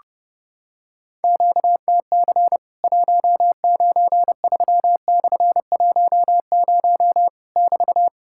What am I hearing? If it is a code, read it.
Morse code, heard as a keyed tone, about 30 wpm, 700 Hz. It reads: QTC 193/10 <BT>
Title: Worked All Europe DX Contest - QTC Exchange - 30wpm